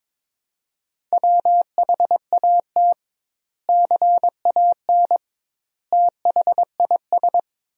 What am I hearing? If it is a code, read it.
Morse code, heard as a keyed tone, about 22 wpm, 700 Hz. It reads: WHAT CAN THIS
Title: 2x Word-Spacing - Sentences from Top 100 Words - 22wpm